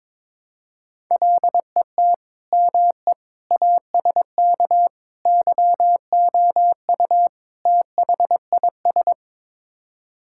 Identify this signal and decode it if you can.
Morse code, heard as a keyed tone, about 22 wpm, 700 Hz. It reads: LET ME ASK YOU THIS
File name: practice / 22wpm / Build A Phrase - 5-Word Phrases - 22wpm